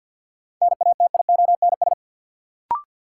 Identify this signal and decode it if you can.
Morse code, heard as a keyed tone, about 52 wpm, 700 Hz. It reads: NATIONS
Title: Sets of 3 Words - New Word Drill - 50wpm